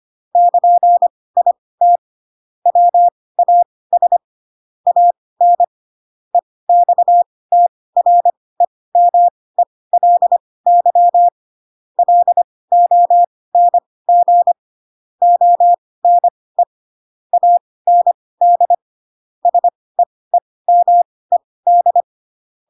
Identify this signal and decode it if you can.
Morse code, heard as a keyed tone, about 25 wpm, spaced as for 18 wpm, 700 Hz. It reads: <KN>IT WAS AN EXTREMELY LONG ONE AND SEEMED
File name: Pinocchio ch3 12 25 ik1hge